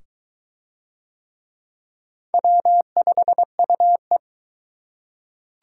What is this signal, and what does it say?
Morse code, heard as a keyed tone, about 21 wpm, 700 Hz. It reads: W5UE